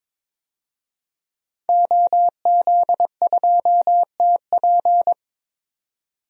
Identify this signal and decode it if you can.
Morse code, heard as a keyed tone, about 22 wpm, 700 Hz. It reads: OZ2TP
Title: DXCC-100 Call Signs - 5 Characters - 22wpm